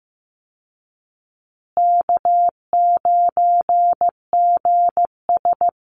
Morse code, heard as a keyed tone, about 14 wpm, 700 Hz. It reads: K9GS
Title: Call Signs Repeated 2x - 15wpm